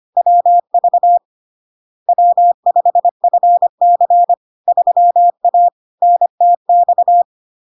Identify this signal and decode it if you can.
Morse code, heard as a keyed tone, about 25 wpm, 700 Hz. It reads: WV W5FC 3A NTX
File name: Log-in-CW-25wpm-700-Hz